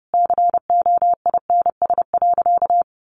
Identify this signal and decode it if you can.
Morse code, heard as a keyed tone, about 30 wpm, 700 Hz. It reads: COINS.
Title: A Sound Of Thunder 30wpm